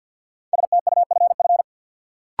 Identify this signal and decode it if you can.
Morse code, heard as a keyed tone, about 51 wpm, 700 Hz. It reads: STUFF
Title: Sentences from Top 1400 Words - New Word Drill - 50wpm